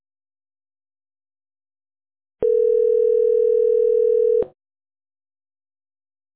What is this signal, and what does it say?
Ringback tone (North America), 440 + 480 Hz.